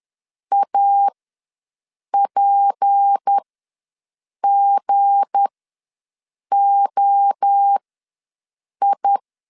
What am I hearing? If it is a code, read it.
Morse code, heard as a keyed tone, about 10 wpm, spaced as for 6 wpm, 800 Hz. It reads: APGOI
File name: LEZIONI / 7 GF9Z / 7 G ws 10wpm 3843